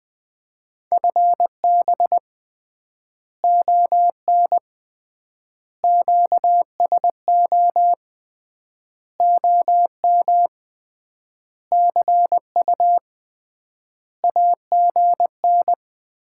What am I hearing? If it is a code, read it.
Morse code, heard as a keyed tone, about 20 wpm, 700 Hz. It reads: FB ON QSO OM CU AGN
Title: Common QSO Phrases - 3x Word-Spacing - 20wpm